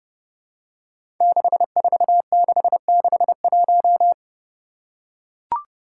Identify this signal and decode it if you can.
Morse code, heard as a keyed tone, about 30 wpm, 700 Hz. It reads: 64661
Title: Five Challenging Letter-Number Combinations 30wpm